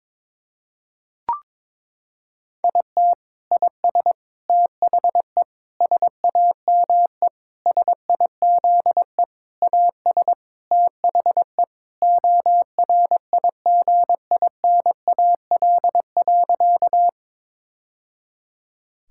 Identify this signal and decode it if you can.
Morse code, heard as a keyed tone, about 21 wpm, 700 Hz. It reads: IT IS THE SAME SIZE AS THE ORIGINAL.
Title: Sentences from Top 1000 Words - Review Part 6 - 22wpm